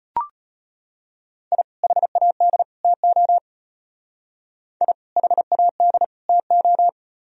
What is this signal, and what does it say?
Morse code, heard as a keyed tone, about 32 wpm, 700 Hz. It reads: I HAD TO I HAD TO